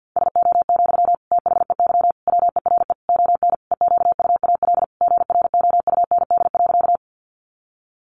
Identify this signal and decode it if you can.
Morse code, heard as a keyed tone, about 50 wpm, 700 Hz. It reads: SO, THEY WERE ON EQUAL GROUND.
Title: Sentences from Top 1400 Words 50wpm